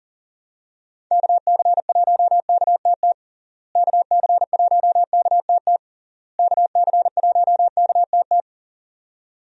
Morse code, heard as a keyed tone, about 40 wpm, 700 Hz. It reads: KC1KTT KC1KTT KC1KTT
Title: USA Call Signs - 6 Characters - Repeated 3x - 40wpm